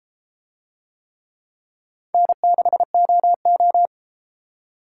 Morse code, heard as a keyed tone, about 31 wpm, 700 Hz. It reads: N6OO